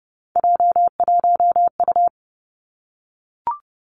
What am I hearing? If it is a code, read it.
Morse code, heard as a keyed tone, about 31 wpm, 700 Hz. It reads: J1U
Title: Three Challenging Letter-Number Combinations 30wpm